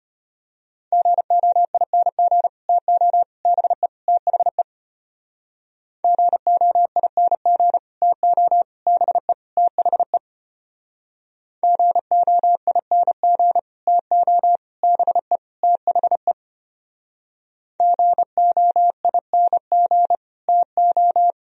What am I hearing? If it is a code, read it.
Morse code, heard as a keyed tone, about 30 wpm, 700 Hz. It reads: GOING TO BE THE GOING TO BE THE GOING TO BE THE GOING TO
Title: Speed-Racing - Sets of 4 Words 25wpm